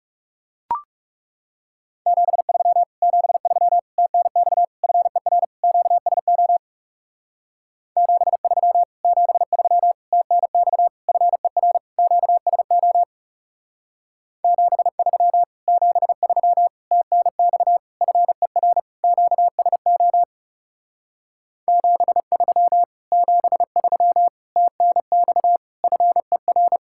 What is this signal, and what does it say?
Morse code, heard as a keyed tone, about 36 wpm, 700 Hz. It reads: 73 73 TNX FER QSO 73 73 TNX FER QSO 73 73 TNX FER QSO 73 73 TNX FER
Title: Speed-Racing Common QSO Phrases Encore - 30wpm